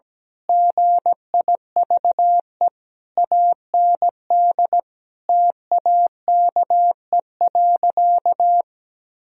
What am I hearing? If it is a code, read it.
Morse code, heard as a keyed tone, about 17 wpm, 700 Hz. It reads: GIVE AND TAKE.
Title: Sentences from Top 1000 Words - Review Part 4 - 17wpm